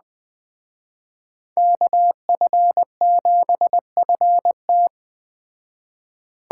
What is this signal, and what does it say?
Morse code, heard as a keyed tone, about 20 wpm, 700 Hz. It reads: KF7FT
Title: Speed Racing - USA 2x2 Call Signs 17wpm